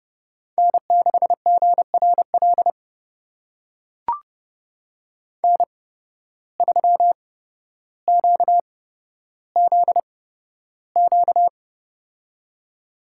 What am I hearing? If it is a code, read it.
Morse code, heard as a keyed tone, about 30 wpm, spaced as for 13 wpm, 700 Hz. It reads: N6GRL N3QZQ
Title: USA Call Signs - 5 Characters - 8x Character Spacing - 30wpm